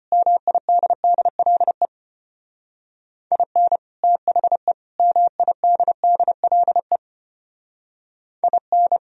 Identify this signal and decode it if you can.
Morse code, heard as a keyed tone, about 31 wpm, 700 Hz. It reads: MIDDLE IN THE MIDDLE IN
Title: Speed-Racing - Sets of 3 Words 25wpm